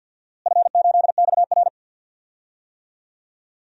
Morse code, heard as a keyed tone, about 51 wpm, 700 Hz. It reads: F8XR
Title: DXCC-100 Call Signs - 4 Characters - Repeated 2x - 50wpm